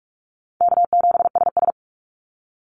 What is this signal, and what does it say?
Morse code, heard as a keyed tone, about 44 wpm, 700 Hz. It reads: K7SS
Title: Call Signs Repeated 2x - 45wpm